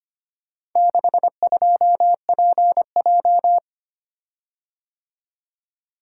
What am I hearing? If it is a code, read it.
Morse code, heard as a keyed tone, about 25 wpm, 700 Hz. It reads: 62PJ
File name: practice / 25wpm / Four Letter-Number Combinations - 25wpm